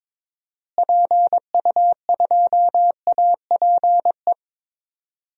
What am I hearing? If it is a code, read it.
Morse code, heard as a keyed tone, about 22 wpm, 700 Hz. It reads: PU2APE
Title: DXCC-100 Call Signs - 6 Characters - Repeated 3x - 22wpm